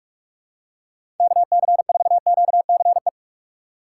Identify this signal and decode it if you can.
Morse code, heard as a keyed tone, about 42 wpm, 700 Hz. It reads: KC4QCE
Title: USA Call Signs - 6 Characters - 4x Character Spacing - 45wpm